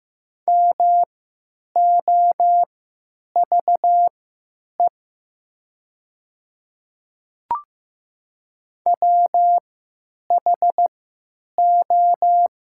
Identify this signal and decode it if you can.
Morse code, heard as a keyed tone, about 15 wpm, spaced as for 9 wpm, 700 Hz. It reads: MOVE WHO
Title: 3x Character Spacing - Top 200 Words - 15wpm